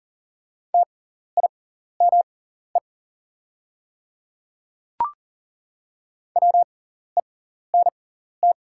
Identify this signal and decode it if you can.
Morse code, heard as a keyed tone, about 40 wpm, spaced as for 14 wpm, 700 Hz. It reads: TIME WENT
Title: 6x Character Spacing - Top 300 Words - 40wpm